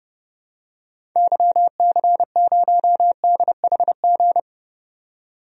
Morse code, heard as a keyed tone, about 30 wpm, 700 Hz. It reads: YC0DHG
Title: DXCC-100 Call Signs - 6 Characters - 30wpm